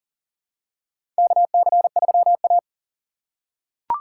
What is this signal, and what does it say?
Morse code, heard as a keyed tone, about 37 wpm, 700 Hz. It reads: KC3A
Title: USA Call Signs - 4 Characters - 3x Character Spacing - 40wpm